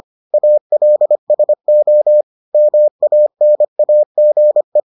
Morse code, heard as a keyed tone, about 25 wpm, 600 Hz. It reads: ALSO MANAGE